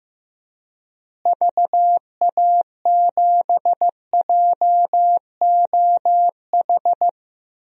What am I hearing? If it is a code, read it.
Morse code, heard as a keyed tone, about 15 wpm, 700 Hz. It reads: VA7JOH
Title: DXCC-100 Call Signs - 6 Characters - Repeated 2x - 15wpm